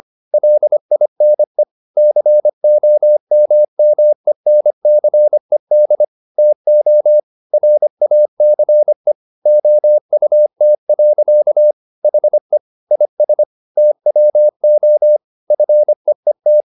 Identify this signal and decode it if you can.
Morse code, heard as a keyed tone, about 25 wpm, 600 Hz. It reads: LINE COMMENCED TO RACE OUT. HE IS TWO FEET
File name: the old man and the sea part 1 25wpm